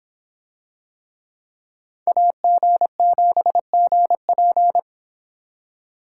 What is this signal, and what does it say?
Morse code, heard as a keyed tone, about 24 wpm, 700 Hz. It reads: AG7GP